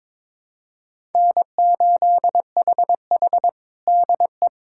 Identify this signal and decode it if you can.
Morse code, heard as a keyed tone, about 22 wpm, 700 Hz. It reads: N8HH DE
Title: USA Callsign DE Callsign - 4 Characters - 22wpm